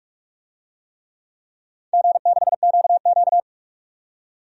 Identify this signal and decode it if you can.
Morse code, heard as a keyed tone, about 44 wpm, 700 Hz. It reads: G6QQ
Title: DXCC-100 Call Signs - 4 Characters - 45wpm